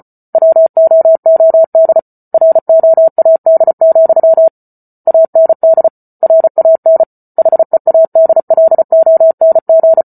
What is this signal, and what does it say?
Morse code, heard as a keyed tone, about 34 wpm, 650 Hz. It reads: WOOD ROAD, AND RAN HEADLONG